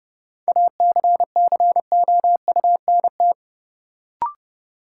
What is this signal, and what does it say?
Morse code, heard as a keyed tone, about 30 wpm, 700 Hz. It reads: ACCOUNT
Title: Sentences from Top 900 Words - New Word Drill - 30wpm